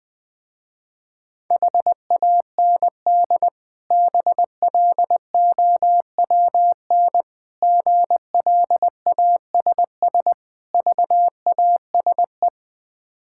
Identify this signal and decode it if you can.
Morse code, heard as a keyed tone, about 20 wpm, 690 Hz. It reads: HAND BLOWN GLASS VASE
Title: Difficult Words with Hints - 20wpm